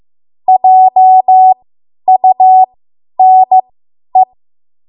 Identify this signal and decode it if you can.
Morse code, heard as a keyed tone, about 15 wpm, spaced as for 10 wpm, 750 Hz. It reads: JUNE
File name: Morse / 10WPM